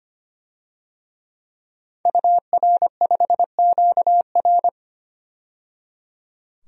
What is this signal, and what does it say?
Morse code, heard as a keyed tone, about 25 wpm, 700 Hz. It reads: UR5QR
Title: DXCC-100 Call Signs - 5 Characters - 25wpm